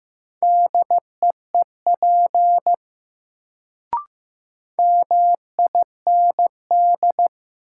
Morse code, heard as a keyed tone, about 15 wpm, 700 Hz. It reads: DEEP MIND
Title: Sentences from Top 500 Words - New Word Drill - 15wpm